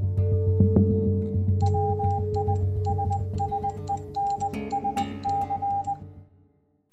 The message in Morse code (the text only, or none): WISSENSC